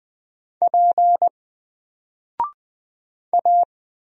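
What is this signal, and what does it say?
Morse code, heard as a keyed tone, about 20 wpm, 700 Hz. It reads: P A
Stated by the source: Single letter and number Morse drill at 20 wpm, from N4LCD.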